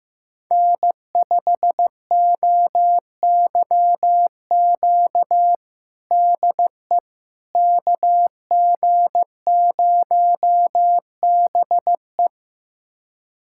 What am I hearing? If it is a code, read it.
Morse code, heard as a keyed tone, about 15 wpm, 700 Hz. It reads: N5OYQ DE KG0BE
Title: USA Callsign DE Callsign 15wpm